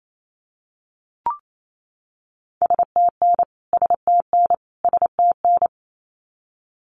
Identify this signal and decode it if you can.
Morse code, heard as a keyed tone, about 28 wpm, 700 Hz. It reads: STN STN STN
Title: Abbreviations Repeated 3x - 28wpm